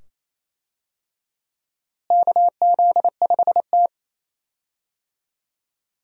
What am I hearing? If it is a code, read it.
Morse code, heard as a keyed tone, about 28 wpm, 700 Hz. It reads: KZ5T
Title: USA Call Signs - 4 Characters - Repeated 2x - 28wpm